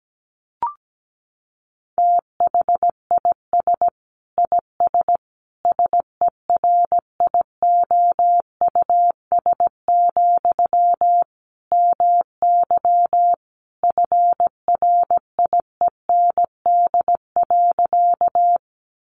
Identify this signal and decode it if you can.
Morse code, heard as a keyed tone, about 17 wpm, 700 Hz. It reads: THIS IS SERIOUS, MY FRIEND.
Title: Sentences from Top 1100 Words 17wpm